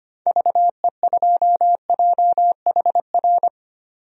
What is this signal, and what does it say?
Morse code, heard as a keyed tone, about 25 wpm, 700 Hz. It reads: VE2JHR
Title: DXCC-100 Call Signs - 6 Characters - 25wpm